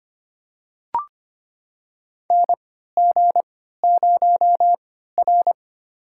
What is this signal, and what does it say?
Morse code, heard as a keyed tone, about 25 wpm, spaced as for 14 wpm, 700 Hz. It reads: NG0R
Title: USA Call Signs - 3x Character Spacing - 25wpm